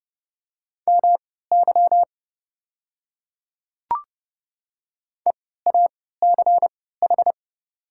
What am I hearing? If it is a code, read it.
Morse code, heard as a keyed tone, about 30 wpm, spaced as for 17 wpm, 690 Hz. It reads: MY EACH
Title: 3x Character Spacing - Top 100 Words - 30wpm